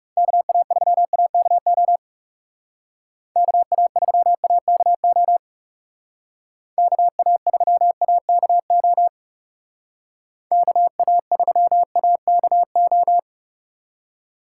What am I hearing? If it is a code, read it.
Morse code, heard as a keyed tone, about 37 wpm, 700 Hz. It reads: KA3AKO KA3AKO KA3AKO KA3AKO
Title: Speed Racing - USA 2x3 Call Signs 30wpm